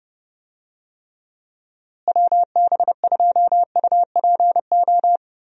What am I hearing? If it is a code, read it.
Morse code, heard as a keyed tone, about 32 wpm, 700 Hz. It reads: WB2UPO
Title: USA Call Signs - 6 Characters - Repeated 3x - 30wpm